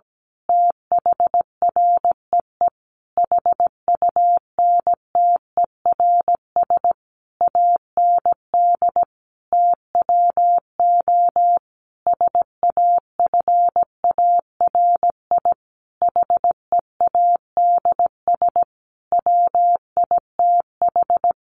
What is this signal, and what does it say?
Morse code, heard as a keyed tone, about 17 wpm, 700 Hz. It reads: THREE HUNTERS AND TWO SAFARI HEADS WITH
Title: A Sound Of Thunder 17wpm